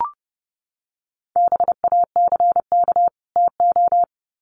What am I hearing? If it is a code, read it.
Morse code, heard as a keyed tone, about 30 wpm, 700 Hz. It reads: BACK TO
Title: Sets of 2 Words 30wpm